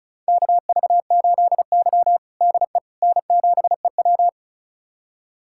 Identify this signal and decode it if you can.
Morse code, heard as a keyed tone, about 36 wpm, 700 Hz. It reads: KV8Y DE N7EW
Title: USA Callsign DE Callsign - 4 Characters - 35wpm